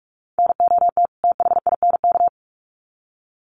Morse code, heard as a keyed tone, about 44 wpm, 700 Hz. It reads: NOT THINK
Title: Sets of 2 Words 45wpm